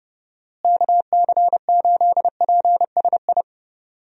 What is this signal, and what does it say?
Morse code, heard as a keyed tone, about 30 wpm, 700 Hz. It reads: KC8PSI
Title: USA Call Signs - 6 Characters - Repeated 3x - 30wpm